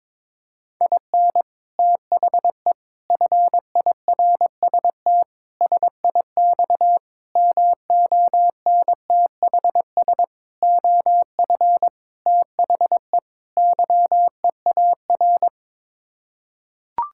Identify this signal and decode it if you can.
Morse code, heard as a keyed tone, about 22 wpm, 700 Hz. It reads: IN THE FIRST SIX MONTHS OF THE YEAR
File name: practice / 22wpm / Sets of 8 Words - 22wpm